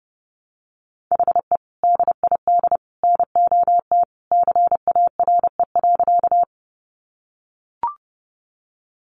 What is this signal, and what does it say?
Morse code, heard as a keyed tone, about 30 wpm, 700 Hz. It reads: HE DID NOT CARE.
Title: Sentences from Top 600 Words 30wpm